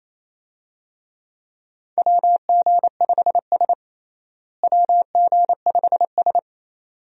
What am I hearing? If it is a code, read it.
Morse code, heard as a keyed tone, about 28 wpm, 700 Hz. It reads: WG5S WG5S
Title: USA Call Signs - 4 Characters - Repeated 2x - 28wpm